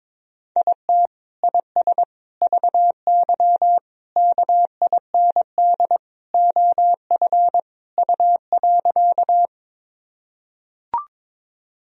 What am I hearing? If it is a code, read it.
Morse code, heard as a keyed tone, about 22 wpm, 700 Hz. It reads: IT IS VY KIND OF U.
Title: Sentences from Top 200 Words and Abbreviations 22wpm